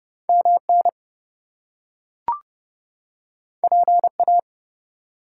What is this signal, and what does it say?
Morse code, heard as a keyed tone, about 30 wpm, 700 Hz. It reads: MN PA
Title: US States and Canadian Provinces 30wpm